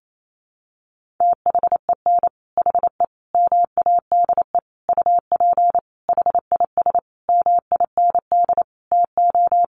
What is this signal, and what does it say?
Morse code, heard as a keyed tone, about 28 wpm, 700 Hz. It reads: THEN HE MADE UP HIS MIND TO
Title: Sentences from Top 1400 Words 28wpm